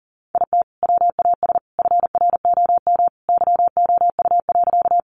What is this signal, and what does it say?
Morse code, heard as a keyed tone, about 37 wpm, 700 Hz. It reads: IT WAS FROM YOU.
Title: Sentences from Top 100 Words 40wpm